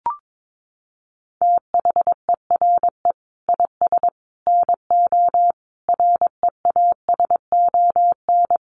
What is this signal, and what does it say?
Morse code, heard as a keyed tone, about 22 wpm, 700 Hz. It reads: THERE IS NO REASON
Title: Sentences from Top 700 Words 22wpm